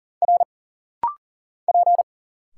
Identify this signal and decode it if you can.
Morse code, heard as a keyed tone, about 40 wpm, 700 Hz. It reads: R P
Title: Single Letters - Rapid-Fire 40wpm